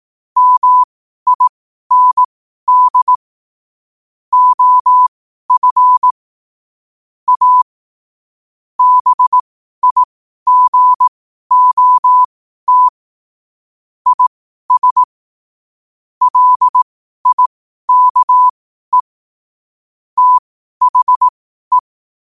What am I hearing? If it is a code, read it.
Morse code, heard as a keyed tone, about 18 wpm, spaced as for 12 wpm, 1000 Hz. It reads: MIND OF A BIGOT IS LIKE THE
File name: qotdfiles / 12wpm